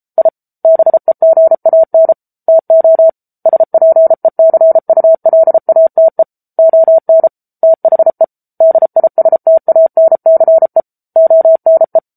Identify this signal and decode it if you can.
Morse code, heard as a keyed tone, about 34 wpm, 650 Hz. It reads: I BEGAN TO SPECULATE ON THE DISTANCE ONE
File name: Book Two / 07-Chapter 7 - 33 WPM